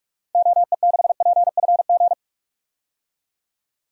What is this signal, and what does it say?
Morse code, heard as a keyed tone, about 43 wpm, 700 Hz. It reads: OE6PFG
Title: DXCC-100 Call Signs - 6 Characters - 45wpm